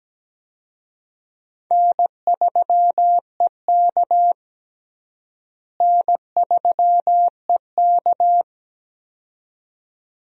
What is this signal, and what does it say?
Morse code, heard as a keyed tone, about 17 wpm, 700 Hz. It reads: N3EK N3EK
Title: USA Call Signs - 4 Characters - Repeated 2x - 17wpm